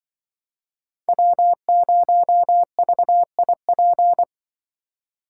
Morse code, heard as a keyed tone, about 24 wpm, 700 Hz. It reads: W0VIP
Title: Speed Racing - USA 1x3 Call Signs 20wpm